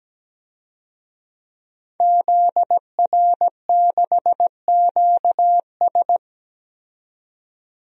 Morse code, heard as a keyed tone, about 17 wpm, 700 Hz. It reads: ZR6QS
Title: DXCC-100 Call Signs - 5 Characters - 17wpm